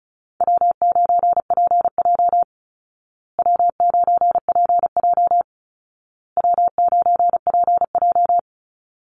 Morse code, heard as a keyed tone, about 34 wpm, 700 Hz. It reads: W9PJ W9PJ W9PJ
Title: Easy Call Signs Repeated 3x - 35wpm